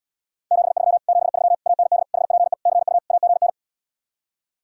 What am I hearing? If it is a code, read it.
Morse code, heard as a keyed tone, about 77 wpm, 700 Hz. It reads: 73 73 AND HPE CU AGN